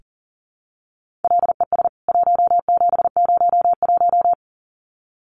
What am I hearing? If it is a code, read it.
Morse code, heard as a keyed tone, about 40 wpm, 700 Hz. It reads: LES 1701
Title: CWT Exchange 40wpm